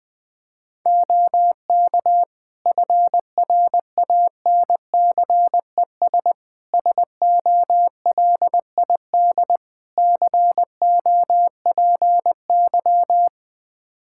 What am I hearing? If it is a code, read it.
Morse code, heard as a keyed tone, about 21 wpm, 700 Hz. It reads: OK FRANCES SOLID COPY
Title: Practice QSO - Exchange 3 - More Info - 20wpm